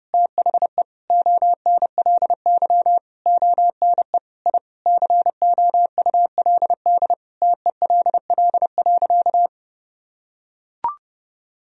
Morse code, heard as a keyed tone, about 32 wpm, 700 Hz. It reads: THE ONLY ONE I COULD TELL.
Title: Sentences from Top 1000 Words - Review Part 4 - 30wpm